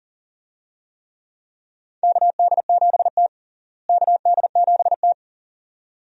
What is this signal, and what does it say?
Morse code, heard as a keyed tone, about 40 wpm, 700 Hz. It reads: KD7T KD7T
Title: USA Call Signs - 4 Characters - Repeated 2x - 40wpm